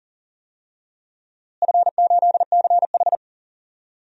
Morse code, heard as a keyed tone, about 40 wpm, 700 Hz. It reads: F8CH
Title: DXCC-100 Call Signs - 4 Characters - Repeated 3x - 40wpm